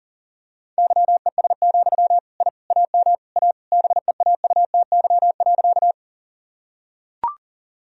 Morse code, heard as a keyed tone, about 39 wpm, 700 Hz. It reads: YES, I AM A BEAUTY.
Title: Sentences from Top 1000 Words - Review Part 2 - 40wpm